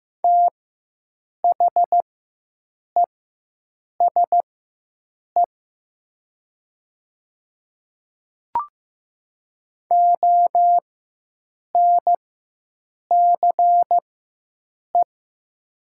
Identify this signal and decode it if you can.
Morse code, heard as a keyed tone, about 15 wpm, spaced as for 7 wpm, 700 Hz. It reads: THESE ONCE
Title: 4x Character Spacing - Top 300 Words - 15wpm